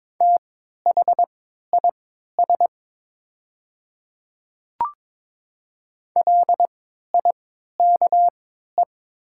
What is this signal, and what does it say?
Morse code, heard as a keyed tone, about 22 wpm, spaced as for 12 wpm, 690 Hz. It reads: THIS LIKE
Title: 3x Character Spacing - Top 100 Words - 22wpm